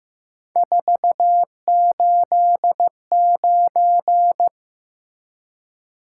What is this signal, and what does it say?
Morse code, heard as a keyed tone, about 15 wpm, 700 Hz. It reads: 489
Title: Top 180 QSO Elements 15wpm